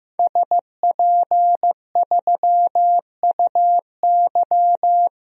Morse code, heard as a keyed tone, about 15 wpm, 700 Hz. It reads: SP3UY
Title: DXCC-100 Call Signs - 5 Characters - 15wpm